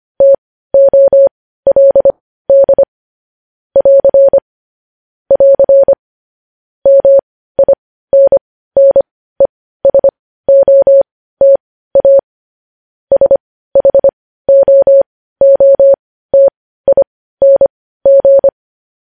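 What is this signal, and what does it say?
Morse code, heard as a keyed tone, about 25 wpm, spaced as for 15 wpm, 550 Hz. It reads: TOLD <AR> <AR> MINNESOTA SHOOTING
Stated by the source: Morse news headlines practice at 15 wpm.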